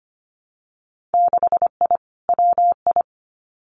Morse code, heard as a keyed tone, about 25 wpm, 700 Hz. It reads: 6I WI
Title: Winter Field Day Exchanges Encore 25wpm